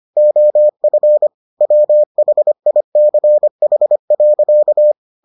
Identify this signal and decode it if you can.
Morse code, heard as a keyed tone, about 25 wpm, 600 Hz. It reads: OF WHICH.